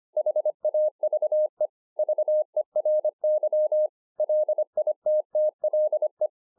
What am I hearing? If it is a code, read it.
Morse code, heard as a keyed tone, about 25 wpm, 610 Hz. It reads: HAVE VERY LITTLE
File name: 2011-04-mp3 / THIS IS NOT MY SHACK 2 25WPM